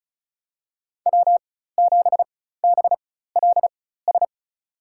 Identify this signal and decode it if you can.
Morse code, heard as a keyed tone, about 34 wpm, spaced as for 16 wpm, 700 Hz. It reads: W7BLS
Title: USA Call Signs - 5 Characters - 4x Character Spacing - 35wpm